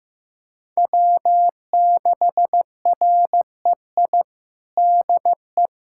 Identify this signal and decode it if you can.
Morse code, heard as a keyed tone, about 15 wpm, 700 Hz. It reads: W6REI DE
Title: USA Callsign DE Callsign 15wpm